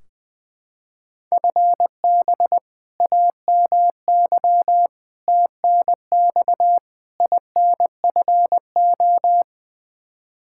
Morse code, heard as a keyed tone, about 20 wpm, 700 Hz. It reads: FB AMY TNX INFO
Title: Practice QSO - Exchange 3 - More Info - 20wpm